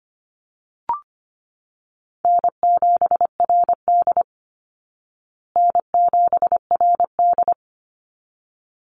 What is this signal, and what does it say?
Morse code, heard as a keyed tone, about 25 wpm, 700 Hz. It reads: N7RD N7RD
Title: Call Signs Repeated 3x - 25wpm